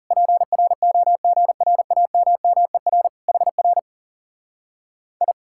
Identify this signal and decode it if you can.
Morse code, heard as a keyed tone, about 40 wpm, 700 Hz. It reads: PROGRAMMER HR I